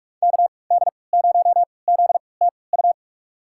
Morse code, heard as a keyed tone, about 46 wpm, spaced as for 26 wpm, 700 Hz. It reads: KD0ZTU
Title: USA Call Signs - 3x Character Spacing - 45wpm